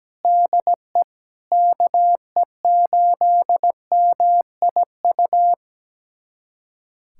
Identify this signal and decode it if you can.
Morse code, heard as a keyed tone, about 17 wpm, 700 Hz. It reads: DE KE8MIU
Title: USA Callsign DE Callsign - 6 Characters - 17wpm